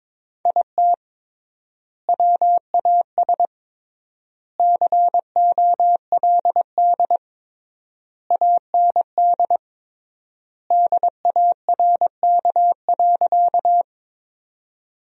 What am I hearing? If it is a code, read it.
Morse code, heard as a keyed tone, about 22 wpm, 700 Hz. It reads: IT WAS COLD AND DARK.
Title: 3x Word-Spacing - Sentences from Top 500 Words - 22wpm